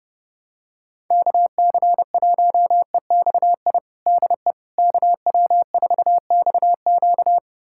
Morse code, heard as a keyed tone, about 30 wpm, 700 Hz. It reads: KC1EXI DE KW4XQ
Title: USA Callsign DE Callsign 30wpm